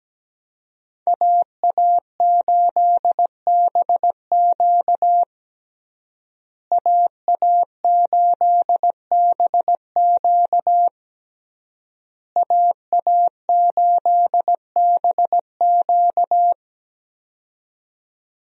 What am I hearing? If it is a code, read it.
Morse code, heard as a keyed tone, about 17 wpm, 700 Hz. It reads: AA8BQ AA8BQ AA8BQ
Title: USA Call Signs - 5 Characters - Repeated 3x - 17wpm